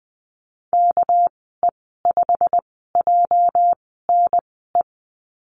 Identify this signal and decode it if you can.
Morse code, heard as a keyed tone, about 19 wpm, spaced as for 14 wpm, 700 Hz. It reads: KE5JNE
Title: 245 Lesson 39g - Letter z - Call Signs 20wpm